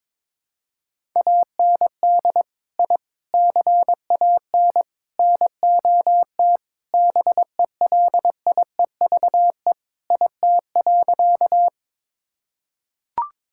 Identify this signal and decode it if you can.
Morse code, heard as a keyed tone, about 22 wpm, 700 Hz. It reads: AND I CAN NOT BELIEVE IT.
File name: practice / 22wpm / Sentences with 4-Word Phrases - 22wpm